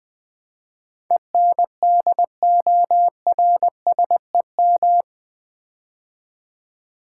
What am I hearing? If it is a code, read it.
Morse code, heard as a keyed tone, about 20 wpm, 700 Hz. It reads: ENDORSEM
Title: Build A Word - 4 Letters At A Time - 11-Letter Words - 20wpm